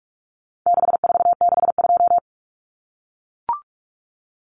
Morse code, heard as a keyed tone, about 44 wpm, 700 Hz. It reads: 6462
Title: Numbers - 4-Digit 45wpm